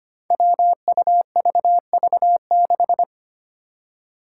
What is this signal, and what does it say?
Morse code, heard as a keyed tone, about 25 wpm, 700 Hz. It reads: WUVV6
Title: Five Challenging Letter-Number Combinations 25wpm